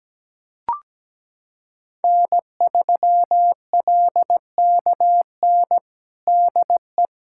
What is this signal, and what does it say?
Morse code, heard as a keyed tone, about 17 wpm, 700 Hz. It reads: N3LKN DE
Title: USA Callsign DE Callsign 17wpm